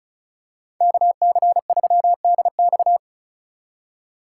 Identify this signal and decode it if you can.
Morse code, heard as a keyed tone, about 34 wpm, 700 Hz. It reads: KC3DX
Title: USA Call Signs - 4x Character Spacing - 35wpm